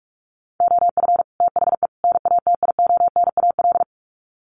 Morse code, heard as a keyed tone, about 44 wpm, 700 Hz. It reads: OF THE NATIONAL
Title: Sets of 3 Words 45wpm